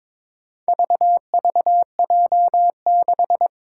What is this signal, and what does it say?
Morse code, heard as a keyed tone, about 22 wpm, 700 Hz. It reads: VVJ6
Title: Four Challenging Letter-Number Combinations 22wpm